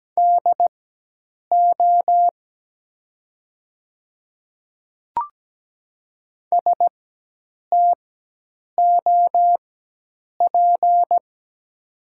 Morse code, heard as a keyed tone, about 17 wpm, spaced as for 8 wpm, 700 Hz. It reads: DO STOP
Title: 4x Character Spacing - Top 300 Words - 17wpm